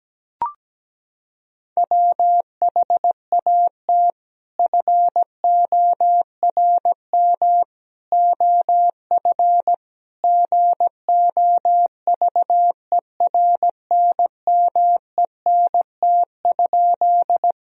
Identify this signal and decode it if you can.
Morse code, heard as a keyed tone, about 17 wpm, 700 Hz. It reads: WHAT FORM OF GOVERNMENT?
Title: Sentences from Top 1000 Words - Review Part 3 - 17wpm